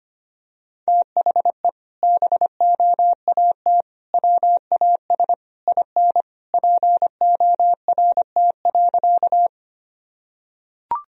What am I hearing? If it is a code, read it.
Morse code, heard as a keyed tone, about 25 wpm, 700 Hz. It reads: THE BOAT WAS IN PORT.
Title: Sentences from Top 1000 Words - Review Part 1 - 25wpm